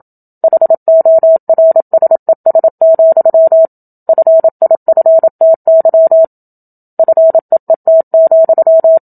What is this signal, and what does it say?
Morse code, heard as a keyed tone, about 27 wpm, 650 Hz. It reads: HORSES, FIFTY FEET,
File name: Book Two / 01-Chapter 1 - 27 WPM